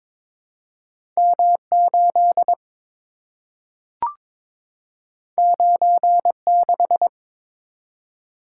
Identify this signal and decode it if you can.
Morse code, heard as a keyed tone, about 22 wpm, 700 Hz. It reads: M8 96
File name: practice / 22wpm / Two Letter-Number Combinations - 22wpm